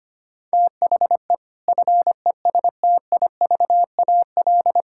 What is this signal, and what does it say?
Morse code, heard as a keyed tone, about 25 wpm, 700 Hz. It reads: THE FESTIVAL
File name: practice / 25wpm / Difficult Sentences with Hints - 25wpm